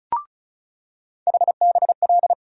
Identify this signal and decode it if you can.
Morse code, heard as a keyed tone, about 34 wpm, 700 Hz. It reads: HBL
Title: USA Call Sign Postfixes - 3 Letters 35wpm